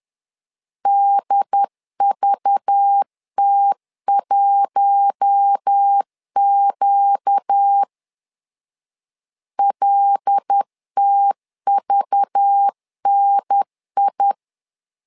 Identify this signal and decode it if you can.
Morse code, heard as a keyed tone, about 10 wpm, 800 Hz. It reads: DVT1Q LTVNI
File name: LEZIONI / 5 LBVC / 5 LBV 10wpm 2248